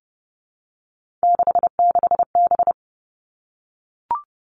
Morse code, heard as a keyed tone, about 30 wpm, 700 Hz. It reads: 66B
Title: Three Challenging Letter-Number Combinations 30wpm